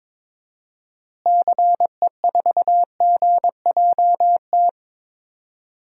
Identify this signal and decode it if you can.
Morse code, heard as a keyed tone, about 21 wpm, 700 Hz. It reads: CE4GJT
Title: DXCC-100 Call Signs - 6 Characters - Repeated 3x - 22wpm